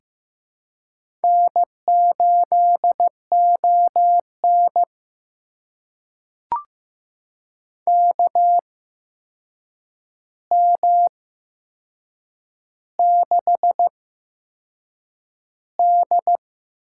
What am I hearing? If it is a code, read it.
Morse code, heard as a keyed tone, about 16 wpm, spaced as for 6 wpm, 700 Hz. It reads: N8ON KM6D
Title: USA Call Signs - 8x Character Spacing - 15wpm